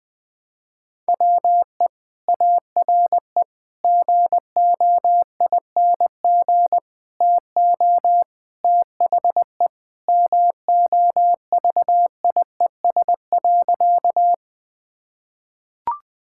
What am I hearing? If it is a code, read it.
Morse code, heard as a keyed tone, about 20 wpm, 700 Hz. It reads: WE ARE GOING TO THE MOVIES.
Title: Sentences with 4-Word Phrases - 20wpm